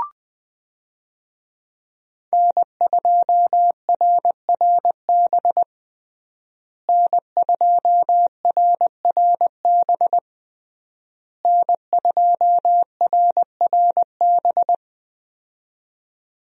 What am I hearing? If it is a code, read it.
Morse code, heard as a keyed tone, about 20 wpm, 700 Hz. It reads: N2RRB N2RRB N2RRB
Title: USA Call Signs - 5 Characters - Repeated 3x - 20wpm